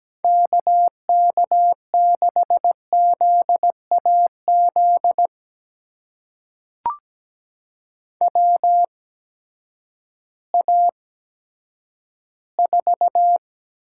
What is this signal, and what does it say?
Morse code, heard as a keyed tone, about 17 wpm, spaced as for 10 wpm, 700 Hz. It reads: KK6ZAZ WA4
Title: USA Call Signs - 6 Characters - 8x Character Spacing - 17wpm